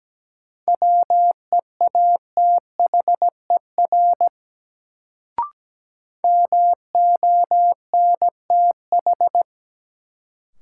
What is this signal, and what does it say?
Morse code, heard as a keyed tone, about 17 wpm, 700 Hz. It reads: WEATHER MONTH
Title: Sentences from Top 800 Words - New Word Drill - 17wpm